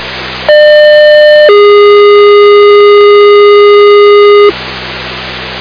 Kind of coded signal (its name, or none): two-tone page